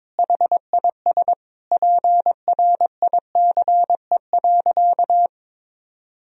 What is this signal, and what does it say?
Morse code, heard as a keyed tone, about 22 wpm, 700 Hz. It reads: HIS PRICE.
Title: Sentences from Top 1000 Words - Review Part 4 - 22wpm